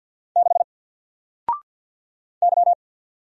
Morse code, heard as a keyed tone, about 48 wpm, 700 Hz. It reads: 6 Y
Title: Single Character (Letter:Number:Punctuation:Prosign) 50wpm